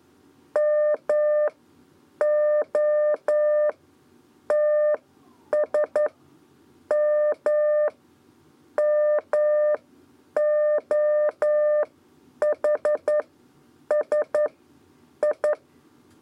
Morse code, heard as a keyed tone, about 11 wpm, spaced as for 7 wpm, 600 Hz. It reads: MOTSMMOHSI
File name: Lezione trasmessa 2 lez 2 exe